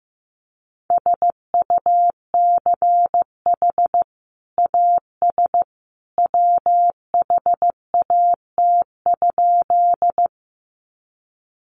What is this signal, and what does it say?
Morse code, heard as a keyed tone, about 15 wpm, 700 Hz. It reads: SUCH AS WHAT?
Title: Sentences from Top 200 Words 15wpm